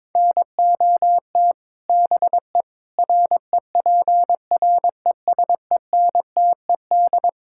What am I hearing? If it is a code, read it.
Morse code, heard as a keyed tone, about 22 wpm, 700 Hz. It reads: NOT BE REPRESENTED
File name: practice / 22wpm / Build A Phrase - 5-Word Phrases - 22wpm